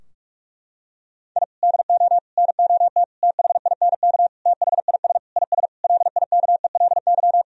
Morse code, heard as a keyed tone, about 46 wpm, 700 Hz. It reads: I DO NOT THINK THIS IS LIKELY
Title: Sentences from Top 1000 Words - Review Part 4 - 45wpm